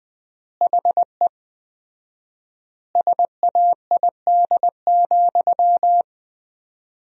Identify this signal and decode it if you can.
Morse code, heard as a keyed tone, about 20 wpm, 700 Hz. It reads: HE SAID,
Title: 4x Word-Spacing - Sentences from Top 500 Words - 20wpm